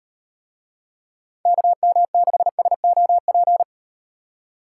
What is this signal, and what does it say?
Morse code, heard as a keyed tone, about 38 wpm, 700 Hz. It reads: KM6SOP